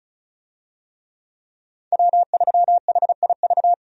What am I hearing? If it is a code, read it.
Morse code, heard as a keyed tone, about 36 wpm, 700 Hz. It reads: W3HIV
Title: USA Call Signs - 5 Characters - 35wpm